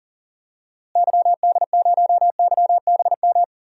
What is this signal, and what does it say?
Morse code, heard as a keyed tone, about 40 wpm, 700 Hz. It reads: YD0YBM
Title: DXCC-100 Call Signs - 6 Characters - Repeated 3x - 40wpm